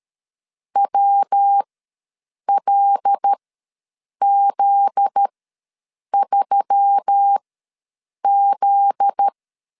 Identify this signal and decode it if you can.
Morse code, heard as a keyed tone, about 13 wpm, spaced as for 7 wpm, 800 Hz. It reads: WLZ3Z
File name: LEZIONI / 7 GF9Z / 7 GF9Z ws 12wpm 3321